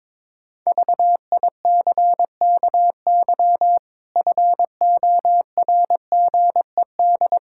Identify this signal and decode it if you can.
Morse code, heard as a keyed tone, about 22 wpm, 700 Hz. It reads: VICKY FORGED
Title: Pangram Challenge v2 - 22wpm